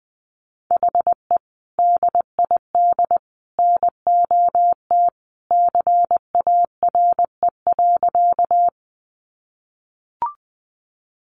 Morse code, heard as a keyed tone, about 20 wpm, 700 Hz. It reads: HE DID NOT CARE.
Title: Sentences from Top 600 Words 20wpm